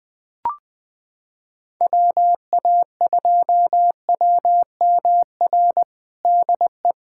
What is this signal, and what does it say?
Morse code, heard as a keyed tone, about 20 wpm, 700 Hz. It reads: WA2WMR DE
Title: USA Callsign DE Callsign - 6 Characters - 20wpm